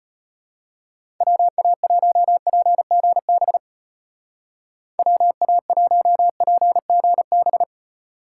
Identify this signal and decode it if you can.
Morse code, heard as a keyed tone, about 31 wpm, 700 Hz. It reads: WA1PGB WA1PGB
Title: Speed Racing - USA 2x3 Call Signs 25wpm